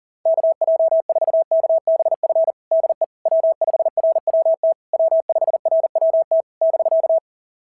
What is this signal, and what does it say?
Morse code, heard as a keyed tone, about 37 wpm, 660 Hz. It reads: KJ4KBF DE W5RWT W5RWT <BK>
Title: SOTA QSO Practice 40wpm